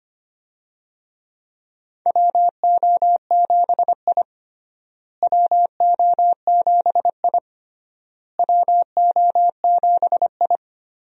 Morse code, heard as a keyed tone, about 25 wpm, 700 Hz. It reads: WO7I WO7I WO7I
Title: USA Call Signs - 4 Characters - Repeated 3x - 25wpm